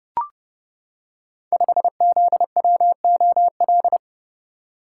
Morse code, heard as a keyed tone, about 30 wpm, 700 Hz. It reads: 5ZWOL
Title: Code Groups - 5 Letter:Numbers - 30wpm